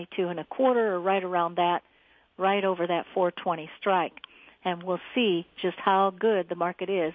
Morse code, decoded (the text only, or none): none